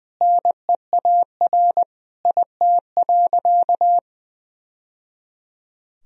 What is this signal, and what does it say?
Morse code, heard as a keyed tone, about 19 wpm, 700 Hz. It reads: NEAR IT.
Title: Sentences from Top 1000 Words - Review Part 1 - 20wpm